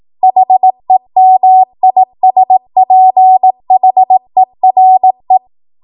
Morse code, heard as a keyed tone, about 18 wpm, 750 Hz. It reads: HEMISPHERE
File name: Morse / 18WPM